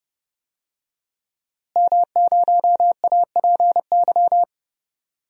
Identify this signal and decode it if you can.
Morse code, heard as a keyed tone, about 32 wpm, 700 Hz. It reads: M0APY
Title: DXCC-100 Call Signs - 5 Characters - Repeated 2x - 30wpm